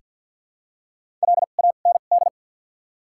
Morse code, heard as a keyed tone, about 47 wpm, spaced as for 35 wpm, 700 Hz. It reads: LAND
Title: 2x Character Spacing - Top 200 Words - 50wpm